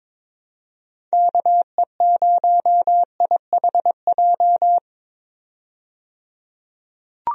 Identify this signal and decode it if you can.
Morse code, heard as a keyed tone, about 22 wpm, 700 Hz. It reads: KE0IHJ
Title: USA Call Signs - 6 Characters - 22wpm